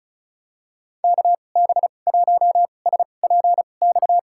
Morse code, heard as a keyed tone, about 34 wpm, spaced as for 25 wpm, 700 Hz. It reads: KB1SPX
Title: USA Call Signs - 2x Character Spacing - 35wpm